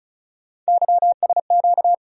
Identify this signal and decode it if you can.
Morse code, heard as a keyed tone, about 36 wpm, 700 Hz. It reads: YSQ
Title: USA Call Sign Postfixes - 3 Letters 35wpm